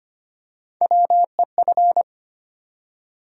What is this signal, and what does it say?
Morse code, heard as a keyed tone, about 25 wpm, 700 Hz. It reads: WEF
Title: USA Call Sign Postfixes - 3 Letters 25wpm